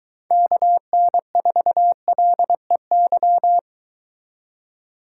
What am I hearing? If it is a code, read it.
Morse code, heard as a keyed tone, about 23 wpm, 700 Hz. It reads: KN4LEY